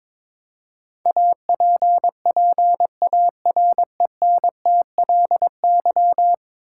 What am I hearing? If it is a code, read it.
Morse code, heard as a keyed tone, about 22 wpm, 700 Hz. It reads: APPARENTLY
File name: practice / 22wpm / 10 Letter Words - First 3 Letters Given - 22wpm